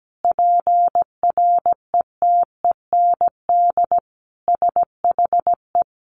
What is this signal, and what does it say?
Morse code, heard as a keyed tone, about 17 wpm, 700 Hz. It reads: PRETEND SHE
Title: Additional Easy Phrases 17wpm